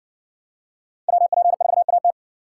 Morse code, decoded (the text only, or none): WP4AT